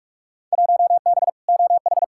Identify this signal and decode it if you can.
Morse code, heard as a keyed tone, about 47 wpm, 700 Hz. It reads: 1B OH
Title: Field Day Exchanges - 45wpm